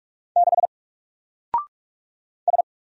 Morse code, heard as a keyed tone, about 46 wpm, 700 Hz. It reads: 6 S
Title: Single Letter-Number Alt 45wpm